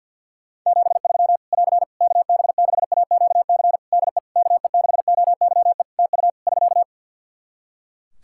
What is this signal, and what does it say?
Morse code, heard as a keyed tone, about 49 wpm, 700 Hz. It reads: 73 <AR> KB6AQC DE KE6OYE TU <SK>
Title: Practice QSO - Exchange 4 - Ending - 50wpm